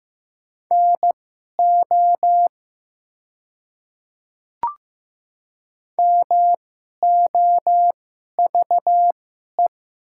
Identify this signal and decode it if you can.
Morse code, heard as a keyed tone, about 15 wpm, spaced as for 11 wpm, 700 Hz. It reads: NO MOVE
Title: 2x Character Spacing - Top 200 Words - 15wpm